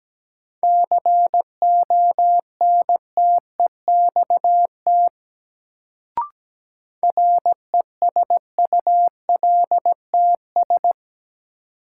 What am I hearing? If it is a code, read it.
Morse code, heard as a keyed tone, about 17 wpm, 700 Hz. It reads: CONTEXT RESULTS
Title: Sets of 4 Words - New Word Drill - 17wpm